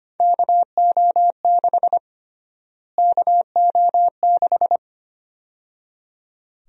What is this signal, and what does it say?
Morse code, heard as a keyed tone, about 25 wpm, 700 Hz. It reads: KO6 KO6
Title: USA Call Sign Prefixes - Repeated 2x - 25wpm